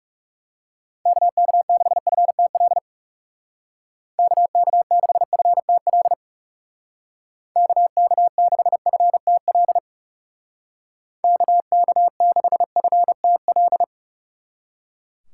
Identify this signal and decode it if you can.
Morse code, heard as a keyed tone, about 37 wpm, 700 Hz. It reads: KK6FTL KK6FTL KK6FTL KK6FTL
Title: Speed Racing - USA 2x3 Call Signs 30wpm